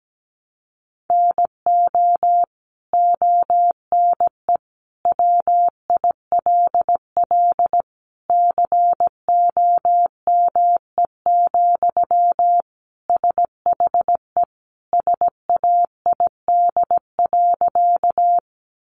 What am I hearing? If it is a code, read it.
Morse code, heard as a keyed tone, about 17 wpm, 700 Hz. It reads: NO ONE WILL COME, SHE SAID.
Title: Sentences from Top 100 Words 17wpm